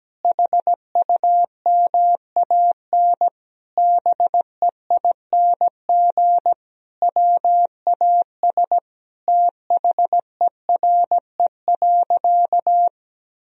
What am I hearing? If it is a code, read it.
Morse code, heard as a keyed tone, about 17 wpm, 700 Hz. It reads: HUMAN BEING WAS THERE.
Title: Sentences from Top 1000 Words - Review Part 4 - 17wpm